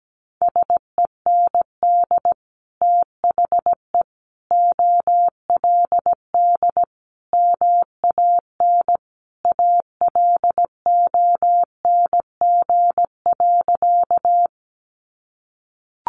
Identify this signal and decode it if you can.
Morse code, heard as a keyed tone, about 17 wpm, 700 Hz. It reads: SEND THE OLD MAN ALONG.
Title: Sentences from Top 1500 Words 17wpm